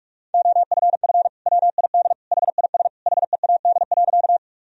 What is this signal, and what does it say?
Morse code, heard as a keyed tone, about 46 wpm, 700 Hz. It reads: OFF WID HIS HEAD.
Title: Sentences from Top 300 Words and Abbreviations 45wpm